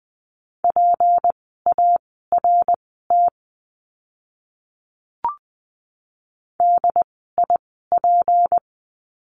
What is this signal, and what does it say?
Morse code, heard as a keyed tone, about 20 wpm, spaced as for 14 wpm, 700 Hz. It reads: PART DIP
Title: 153 Lesson 25f - Letter p - Words 20wpm